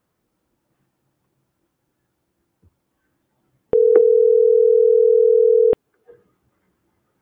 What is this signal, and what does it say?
Ringback tone (North America), 440 + 480 Hz.